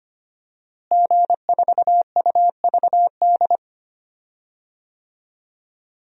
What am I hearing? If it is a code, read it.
Morse code, heard as a keyed tone, about 25 wpm, 700 Hz. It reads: G4UVD
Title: DXCC-100 Call Signs - 5 Characters - Repeated 2x - 25wpm